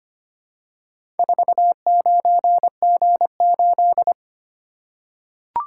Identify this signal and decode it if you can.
Morse code, heard as a keyed tone, about 25 wpm, 700 Hz. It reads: 49G8
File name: practice / 25wpm / Four Letter-Number Combinations - 25wpm